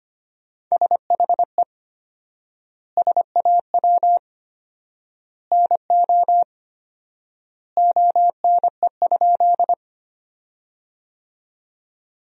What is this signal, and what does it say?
Morse code, heard as a keyed tone, about 25 wpm, 700 Hz. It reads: SHE SAW NO ONE?
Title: 4x Word-Spacing - Sentences from Top 300 Words - 25wpm